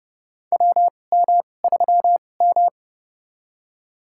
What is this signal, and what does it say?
Morse code, heard as a keyed tone, about 31 wpm, spaced as for 22 wpm, 700 Hz. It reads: WM3M
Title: USA Call Signs - 4 Characters - 2x Character Spacing - 30wpm